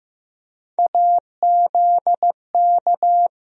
Morse code, heard as a keyed tone, about 15 wpm, 700 Hz. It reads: AZK
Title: USA Call Sign Postfixes - 3 Letters 15wpm